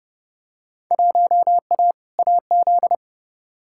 Morse code, heard as a keyed tone, about 32 wpm, 700 Hz. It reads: 1A AZ
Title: Field Day Exchanges - 30wpm